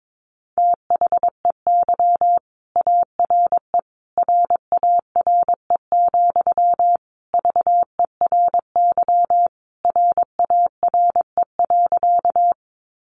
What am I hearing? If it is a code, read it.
Morse code, heard as a keyed tone, about 22 wpm, 700 Hz. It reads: THEY ARE RARE, VERY RARE.
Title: Sentences from Top 1100 Words 22wpm